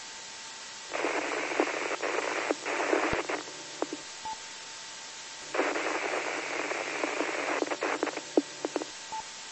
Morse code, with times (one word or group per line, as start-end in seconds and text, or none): none